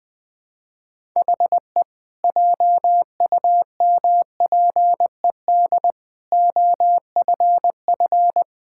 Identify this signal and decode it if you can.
Morse code, heard as a keyed tone, about 19 wpm, 700 Hz. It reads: HE JUMPED OFF
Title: Sentences with 2-Word Phrases - 20wpm